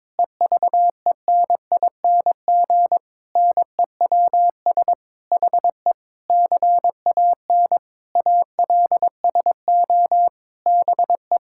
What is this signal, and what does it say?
Morse code, heard as a keyed tone, about 22 wpm, 700 Hz. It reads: EVENING NEWS HE CAN ALSO BE
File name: Sets of 9 Words - 22wpm